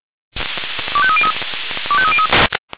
Five-tone selective call (ZVEI), heard as quick runs of five tones, tones of about 70 ms. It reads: 25303 25303